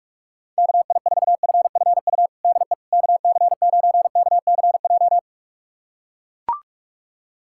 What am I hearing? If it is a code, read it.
Morse code, heard as a keyed tone, about 41 wpm, 700 Hz. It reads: KI4FFU DE KC9KCJ
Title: USA Callsign DE Callsign 45wpm